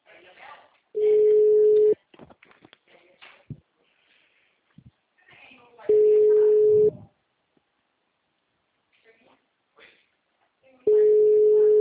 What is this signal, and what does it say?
Ringback tone (Europe, 425 Hz), with speech.